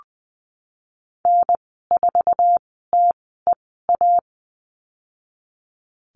Morse code, heard as a keyed tone, about 20 wpm, 700 Hz. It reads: N 4 T E A
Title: 076 Lesson 13g - Letter L - Call Signs 20wpm